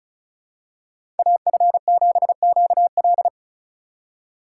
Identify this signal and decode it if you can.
Morse code, heard as a keyed tone, about 34 wpm, 700 Hz. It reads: AF7QL
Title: USA Call Signs - 2x Character Spacing - 35wpm